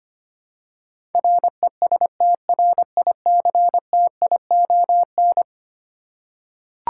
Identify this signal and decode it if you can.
Morse code, heard as a keyed tone, about 25 wpm, 700 Hz. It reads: RESTRICTION
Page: morse-code-ninja-practice-25wpm